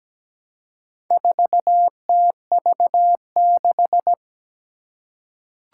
Morse code, heard as a keyed tone, about 17 wpm, 700 Hz. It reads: 4TV6
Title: Four Challenging Letter-Number Combinations 17wpm